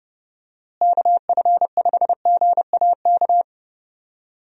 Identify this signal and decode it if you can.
Morse code, heard as a keyed tone, about 30 wpm, 700 Hz. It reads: KF5GAK
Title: USA Call Signs - 6 Characters - Repeated 3x - 30wpm